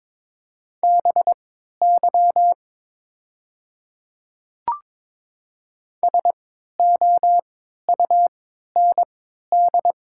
Morse code, heard as a keyed tone, about 22 wpm, spaced as for 13 wpm, 700 Hz. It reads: BY SOUND
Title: 3x Character Spacing - Top 100 Words - 22wpm